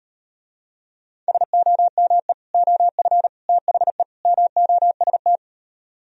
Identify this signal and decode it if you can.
Morse code, heard as a keyed tone, about 38 wpm, 700 Hz. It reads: SOME OF THE MOST